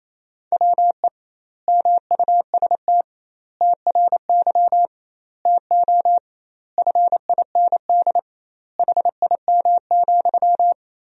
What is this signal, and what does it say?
Morse code, heard as a keyed tone, about 28 wpm, 700 Hz. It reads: WE MUST TRY TO FIND HIM,
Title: 2x Word-Spacing - Sentences from Top 400 Words - 28wpm